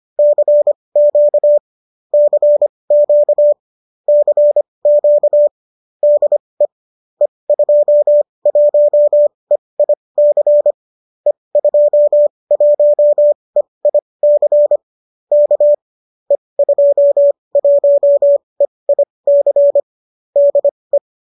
Morse code, heard as a keyed tone, about 25 wpm, spaced as for 20 wpm, 570 Hz. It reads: CQ CQ CQ DE E21EIC E21EIC K E21EIC DE
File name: qso208 20